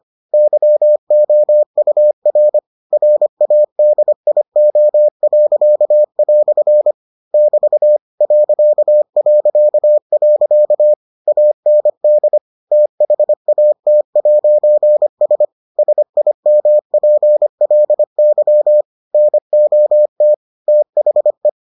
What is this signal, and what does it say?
Morse code, heard as a keyed tone, about 25 wpm, 600 Hz. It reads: YOUR RADIO." <BT> ... AND THAT'S SIMPLY NOT THE